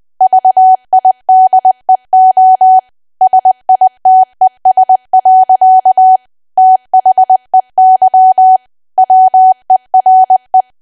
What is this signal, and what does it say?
Morse code, heard as a keyed tone, about 20 wpm, 750 Hz. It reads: VIDEO SITES. THEY WERE
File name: Archive / 20 WPM / 20WPM